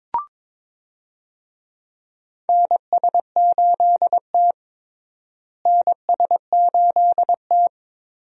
Morse code, heard as a keyed tone, about 22 wpm, 700 Hz. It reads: NS8T NS8T
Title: USA Call Signs - 4 Characters - Repeated 2x - 22wpm